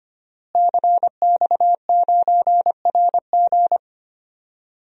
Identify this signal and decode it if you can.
Morse code, heard as a keyed tone, about 25 wpm, 700 Hz. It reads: CX9RG
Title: DXCC-100 Call Signs - 5 Characters - Repeated 2x - 25wpm